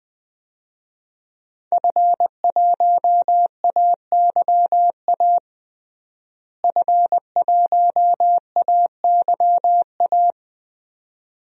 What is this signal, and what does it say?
Morse code, heard as a keyed tone, about 20 wpm, 700 Hz. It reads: F1AYA F1AYA
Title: DXCC-100 Call Signs - 5 Characters - Repeated 2x - 20wpm